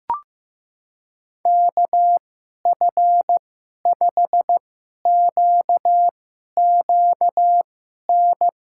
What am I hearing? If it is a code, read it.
Morse code, heard as a keyed tone, about 16 wpm, spaced as for 11 wpm, 700 Hz. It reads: KF5QQN
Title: USA Call Signs - 6 Characters - 2x Character Spacing - 15wpm